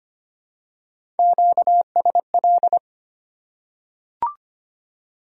Morse code, heard as a keyed tone, about 25 wpm, 700 Hz. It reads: QSL
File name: Top 90 QSO Elements - 8x Character Spacing - 25wpm